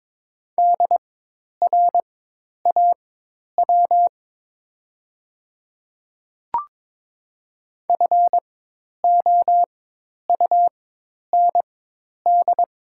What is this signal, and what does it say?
Morse code, heard as a keyed tone, about 22 wpm, spaced as for 10 wpm, 700 Hz. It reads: DRAW FOUND
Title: 4x Character Spacing - Top 300 Words - 22wpm